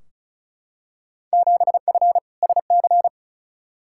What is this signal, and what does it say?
Morse code, heard as a keyed tone, about 35 wpm, 700 Hz. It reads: 7F SC
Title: Field Day Exchanges Encore - 35wpm